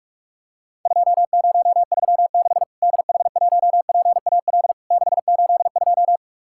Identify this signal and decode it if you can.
Morse code, heard as a keyed tone, about 46 wpm, 700 Hz. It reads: 2036 DH1PAL 682
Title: Worked All Europe DX Contest - QTC Exchange - 45wpm